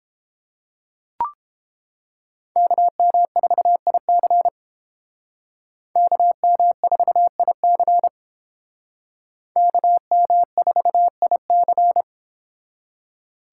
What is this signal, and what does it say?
Morse code, heard as a keyed tone, about 30 wpm, 690 Hz. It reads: KM4IC KM4IC KM4IC